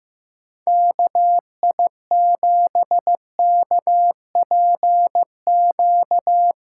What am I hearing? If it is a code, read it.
Morse code, heard as a keyed tone, about 15 wpm, 700 Hz. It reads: KI7KPQ
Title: USA Callsign DE Callsign - 6 Characters - 15wpm